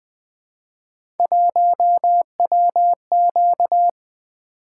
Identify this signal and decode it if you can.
Morse code, heard as a keyed tone, about 20 wpm, 700 Hz. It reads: 1WQ
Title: Three Letter-Number Combinations - 20wpm